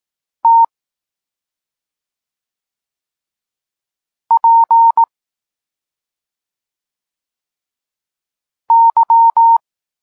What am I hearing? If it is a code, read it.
Morse code, heard as a keyed tone, about 18 wpm, 920 Hz. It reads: T P Y
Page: Tekst 32 – Doskonalenie odbioru poznanych liter